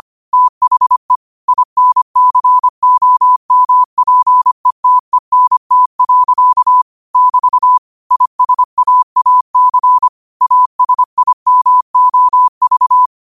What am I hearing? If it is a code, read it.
Morse code, heard as a keyed tone, about 25 wpm, 1000 Hz. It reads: THE INCOMPETENT. <BT> ISAAC ASIMOV
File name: qotdfiles / 25wpm